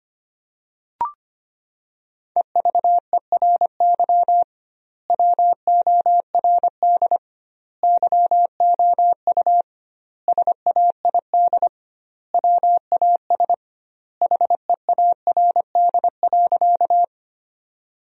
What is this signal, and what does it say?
Morse code, heard as a keyed tone, about 25 wpm, 700 Hz. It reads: EVERY WORD YOU SAID WAS HEARD.
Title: 2x Word-Spacing - Sentences from Top 400 Words - 25wpm